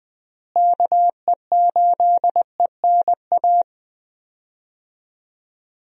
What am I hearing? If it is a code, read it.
Morse code, heard as a keyed tone, about 21 wpm, 700 Hz. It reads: KE8ENA
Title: USA Call Signs - 6 Characters - Repeated 3x - 20wpm